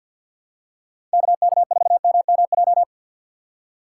Morse code, heard as a keyed tone, about 50 wpm, 700 Hz. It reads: KK4MMJ
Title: USA Call Signs - 6x Character Spacing - 50wpm